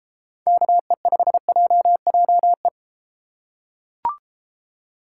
Morse code, heard as a keyed tone, about 33 wpm, 700 Hz. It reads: KE5JJE